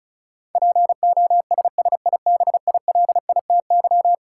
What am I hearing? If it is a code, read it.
Morse code, heard as a keyed tone, about 36 wpm, 700 Hz. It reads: POSSIBILITY
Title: Sets of 3 Words - New Word Drill - 35wpm